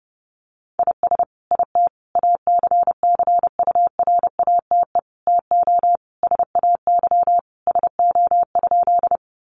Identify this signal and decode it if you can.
Morse code, heard as a keyed tone, about 28 wpm, 700 Hz. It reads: IS IT ACCURATE TO SAY SO?
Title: Sentences from Top 1200 Words 30wpm